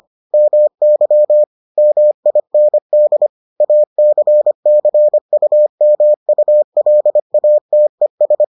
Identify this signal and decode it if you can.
Morse code, heard as a keyed tone, about 25 wpm, 600 Hz. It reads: MY MIND ACCUMULATES